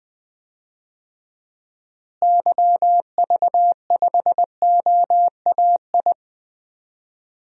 Morse code, heard as a keyed tone, about 20 wpm, 700 Hz. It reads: YV5OAI
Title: DXCC-100 Call Signs - 6 Characters - Repeated 3x - 20wpm